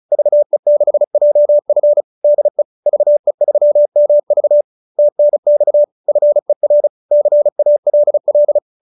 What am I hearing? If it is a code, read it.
Morse code, heard as a keyed tone, about 34 wpm, 590 Hz. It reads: VE6JF DE VE3MV TNX FER CALL